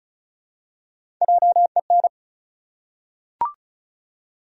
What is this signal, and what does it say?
Morse code, heard as a keyed tone, about 34 wpm, 700 Hz. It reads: JEN
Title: USA Common Names - 35wpm